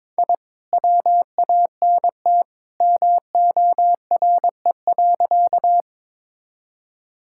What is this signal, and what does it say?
Morse code, heard as a keyed tone, about 22 wpm, 700 Hz. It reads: I WANT MORE.
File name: practice / 22wpm / Sentences with 2-Word Phrases - 22wpm